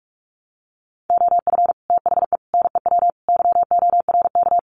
Morse code, heard as a keyed tone, about 44 wpm, 700 Hz. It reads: OF THE NEW YORK
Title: Sets of 4 Words 45wpm